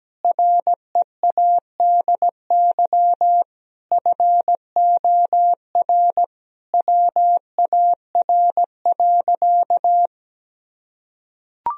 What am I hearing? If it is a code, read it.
Morse code, heard as a keyed tone, about 17 wpm, 700 Hz. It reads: READY FOR WAR.
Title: Sentences from Top 1000 Words - Review Part 3 - 17wpm